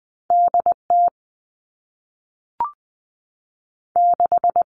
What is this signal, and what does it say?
Morse code, heard as a keyed tone, about 20 wpm, 700 Hz. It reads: D T 6
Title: Two Challenging Letter-Number Combinations 20wpm